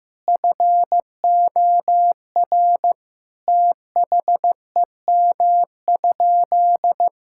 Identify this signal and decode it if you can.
Morse code, heard as a keyed tone, about 16 wpm, 700 Hz. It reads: FOR THEM?
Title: Sentences from Top 1000 Words - Review Part 5 - 15wpm